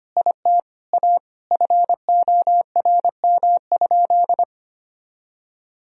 Morse code, heard as a keyed tone, about 25 wpm, 700 Hz. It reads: IT A FORM?
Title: Sentences from Top 1000 Words - Review Part 1 - 25wpm